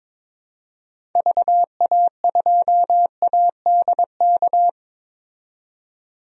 Morse code, heard as a keyed tone, about 22 wpm, 700 Hz. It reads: VA2ADK
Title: DXCC-100 Call Signs - 6 Characters - 22wpm